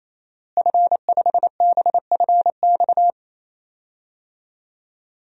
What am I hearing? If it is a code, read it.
Morse code, heard as a keyed tone, about 27 wpm, 700 Hz. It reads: F5BFX
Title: DXCC-100 Call Signs - 5 Characters - Repeated 3x - 28wpm